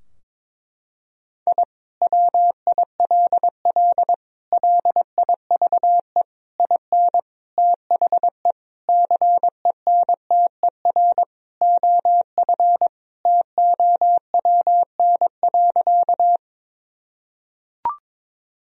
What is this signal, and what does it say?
Morse code, heard as a keyed tone, about 22 wpm, 700 Hz. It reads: I WILL LIVE IN THE CENTER OF TOWN.
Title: Sentences from Top 1000 Words - Review Part 2 - 22wpm